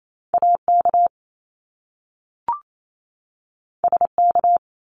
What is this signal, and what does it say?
Morse code, heard as a keyed tone, about 28 wpm, 700 Hz. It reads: AK SK
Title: ARRL Sections 28wpm